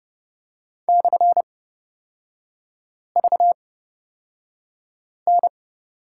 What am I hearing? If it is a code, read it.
Morse code, heard as a keyed tone, about 30 wpm, spaced as for 5 wpm, 700 Hz. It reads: /VN
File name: Lesson 38e - Letter q - Three Characters 30wpm